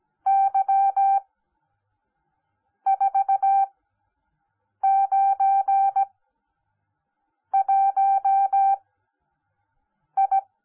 Morse code, heard as a keyed tone, about 17 wpm, spaced as for 5 wpm, 780 Hz. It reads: Y491I
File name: Onlinelehrgang / morsen / DJ4UF Morsen10